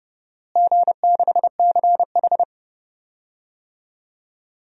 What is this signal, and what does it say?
Morse code, heard as a keyed tone, about 30 wpm, 700 Hz. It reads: G6CH
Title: DXCC-100 Call Signs - 4 Characters - Repeated 2x - 30wpm